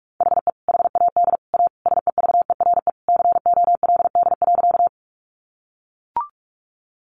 Morse code, heard as a keyed tone, about 44 wpm, 700 Hz. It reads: HE HAD A SEVERE COLD.
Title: Sentences from Top 1400 Words 45wpm